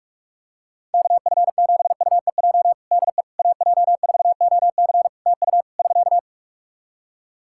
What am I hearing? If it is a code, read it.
Morse code, heard as a keyed tone, about 46 wpm, 700 Hz. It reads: KF7UEJ DE AJ4OC TU <SK>
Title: Practice QSO - Exchange 4 - Ending - 45wpm